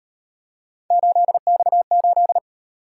Morse code, heard as a keyed tone, about 37 wpm, 700 Hz. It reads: 8X8